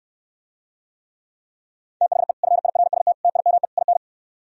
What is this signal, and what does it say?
Morse code, heard as a keyed tone, about 67 wpm, 700 Hz. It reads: THE FIRST TIME IN